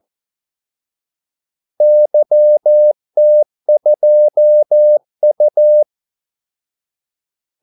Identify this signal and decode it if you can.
Morse code, heard as a keyed tone, about 14 wpm, 600 Hz. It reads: YT2U
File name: Callsigns-14wpm